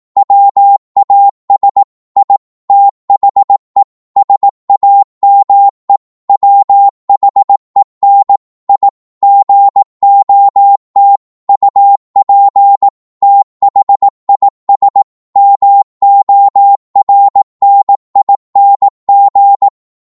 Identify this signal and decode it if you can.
Morse code, heard as a keyed tone, about 18 wpm, 800 Hz. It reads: WAS I THE SAME WHEN I GOT UP THIS MORNING
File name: Morse / psg-18wpm